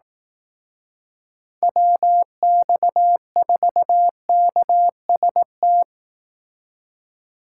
Morse code, heard as a keyed tone, about 18 wpm, 700 Hz. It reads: WX4KST
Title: Speed Racing - USA Call Signs 15wpm